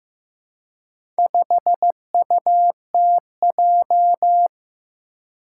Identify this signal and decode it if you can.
Morse code, heard as a keyed tone, about 15 wpm, 700 Hz. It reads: 5UTJ
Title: Four Challenging Letter-Number Combinations 15wpm